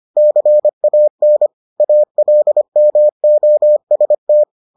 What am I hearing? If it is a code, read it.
Morse code, heard as a keyed tone, about 25 wpm, 600 Hz. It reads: CAN ALMOST